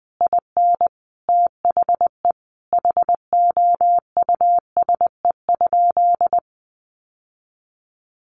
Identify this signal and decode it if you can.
Morse code, heard as a keyed tone, about 20 wpm, 700 Hz. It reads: IN THE HOUSE?
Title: Sentences from Top 1100 Words 20wpm